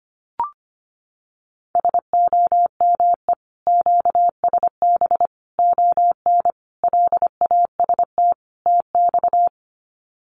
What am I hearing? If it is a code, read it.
Morse code, heard as a keyed tone, about 25 wpm, 700 Hz. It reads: SOME QSB ON LAST TX
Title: Common QSO Phrases 25wpm